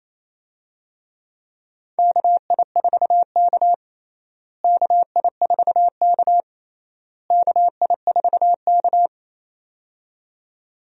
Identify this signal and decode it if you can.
Morse code, heard as a keyed tone, about 28 wpm, 700 Hz. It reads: KI4K KI4K KI4K
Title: USA Call Signs - 4 Characters - Repeated 3x - 28wpm